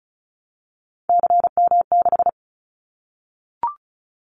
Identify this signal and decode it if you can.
Morse code, heard as a keyed tone, about 36 wpm, 700 Hz. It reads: CM6
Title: Callsign Prefixes for Top 100 DXCC - 35wpm